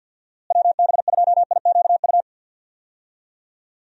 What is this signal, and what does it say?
Morse code, heard as a keyed tone, about 52 wpm, 700 Hz. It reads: WB2IQU
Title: USA Call Signs - 6 Characters - Repeated 3x - 50wpm